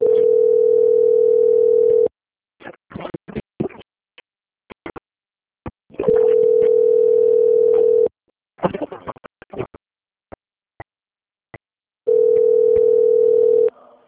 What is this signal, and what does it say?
Ringback tone (North America), 440 + 480 Hz.
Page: Brasilia - DF